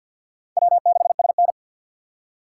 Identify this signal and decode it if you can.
Morse code, heard as a keyed tone, about 52 wpm, 690 Hz. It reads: W6SN
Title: USA Call Signs - 6x Character Spacing - 50wpm